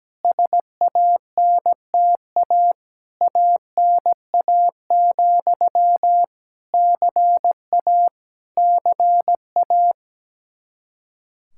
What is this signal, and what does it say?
Morse code, heard as a keyed tone, about 17 wpm, 700 Hz. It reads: SANTA ANA, CA CA
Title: Practice QSO - Exchange 1 - Beginning - 17wpm